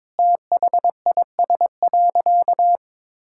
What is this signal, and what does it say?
Morse code, heard as a keyed tone, about 22 wpm, 700 Hz. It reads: THIS.
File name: practice / 22wpm / Sentences with 3-Word Phrases - 22wpm